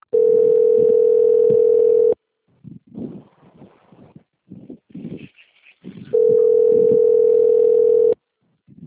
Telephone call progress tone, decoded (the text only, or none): ringback tone